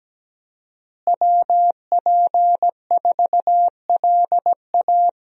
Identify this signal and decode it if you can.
Morse code, heard as a keyed tone, about 17 wpm, 700 Hz. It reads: WP4LA
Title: USA Call Signs - 5 Characters - Repeated 3x - 17wpm